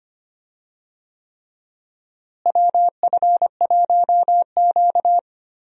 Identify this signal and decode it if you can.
Morse code, heard as a keyed tone, about 25 wpm, 700 Hz. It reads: WF1Q
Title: USA Call Signs - 4 Characters - Repeated 3x - 25wpm